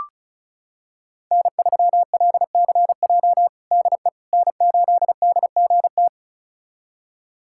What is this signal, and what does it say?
Morse code, heard as a keyed tone, about 36 wpm, 700 Hz. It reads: N3LCJ DE N8DGT
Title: USA Callsign DE Callsign 35wpm